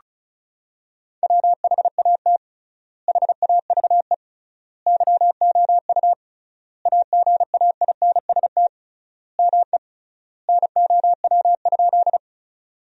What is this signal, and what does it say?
Morse code, heard as a keyed tone, about 36 wpm, 700 Hz. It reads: WHAT HAVE YOU AGAINST ME NOW?
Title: 3x Word-Spacing - Sentences from Top 400 Words - 35wpm